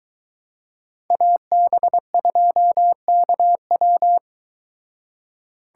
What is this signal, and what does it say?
Morse code, heard as a keyed tone, about 22 wpm, 700 Hz. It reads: AB2KW